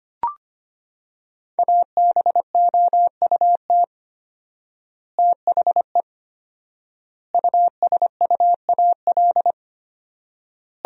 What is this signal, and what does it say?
Morse code, heard as a keyed tone, about 25 wpm, 700 Hz. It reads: ABOUT THE USUAL
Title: 4x Word-Spacing - Sentences from Top 400 Words - 25wpm